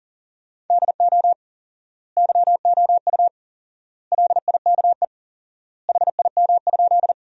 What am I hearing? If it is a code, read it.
Morse code, heard as a keyed tone, about 40 wpm, 700 Hz. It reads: DO YOU LIKE HIM?
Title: 4x Word-Spacing - Sentences from Top 100 Words - 40wpm